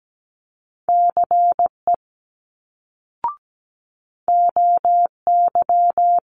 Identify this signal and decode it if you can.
Morse code, heard as a keyed tone, about 17 wpm, 700 Hz. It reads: CE OY
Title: Multi-letter Phonemes Quick Fire 17wpm